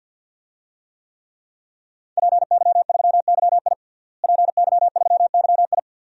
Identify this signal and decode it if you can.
Morse code, heard as a keyed tone, about 52 wpm, 700 Hz. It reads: PY3YI PY3YI
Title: DXCC-100 Call Signs - 5 Characters - Repeated 3x - 50wpm